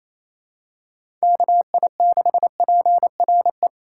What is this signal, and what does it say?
Morse code, heard as a keyed tone, about 28 wpm, 700 Hz. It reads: KI6PRE
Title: USA Call Signs - 6 Characters - 4x Character Spacing - 28wpm